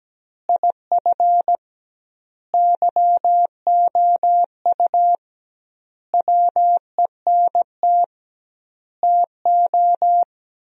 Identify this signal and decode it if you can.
Morse code, heard as a keyed tone, about 17 wpm, 700 Hz. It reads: IF YOU WENT TO
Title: 2x Word-Spacing - Sentences from Top 400 Words - 17wpm